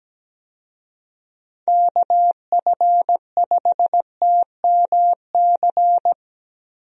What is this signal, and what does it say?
Morse code, heard as a keyed tone, about 17 wpm, 700 Hz. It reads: KF5TMC
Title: USA Call Signs - 6 Characters - Repeated 3x - 17wpm